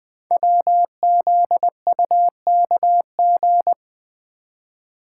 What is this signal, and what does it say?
Morse code, heard as a keyed tone, about 20 wpm, 700 Hz. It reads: WZUKG
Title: Code Groups - 5 Letters - 20wpm